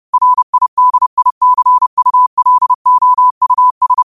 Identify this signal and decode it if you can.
Morse code, heard as a keyed tone, about 30 wpm, 1000 Hz. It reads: RIDICULOUS
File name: qotdfiles / 30wpm